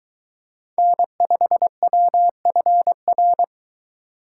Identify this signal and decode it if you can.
Morse code, heard as a keyed tone, about 22 wpm, 700 Hz. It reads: N5WFR